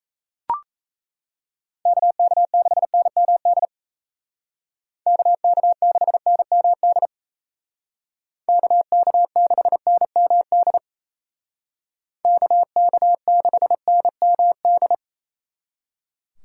Morse code, heard as a keyed tone, about 33 wpm, 700 Hz. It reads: KK6NMD KK6NMD KK6NMD KK6NMD
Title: Speed Racing - USA Call Signs 28wpm